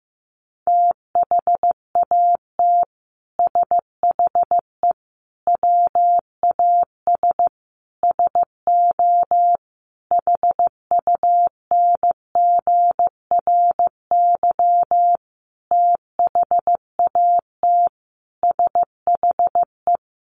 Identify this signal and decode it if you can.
Morse code, heard as a keyed tone, about 15 wpm, 700 Hz. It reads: THAT SHE WAS SO HUNGRY THAT SHE
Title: Sentences from Top 1400 Words 15wpm